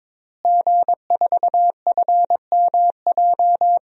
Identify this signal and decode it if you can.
Morse code, heard as a keyed tone, about 22 wpm, 700 Hz. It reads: G4FMJ
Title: DXCC-100 Call Signs - 5 Characters - Repeated 3x - 22wpm